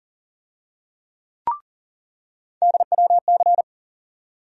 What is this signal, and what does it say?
Morse code, heard as a keyed tone, about 40 wpm, 700 Hz. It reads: DWC
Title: USA Call Sign Postfixes - 3 Letters 40wpm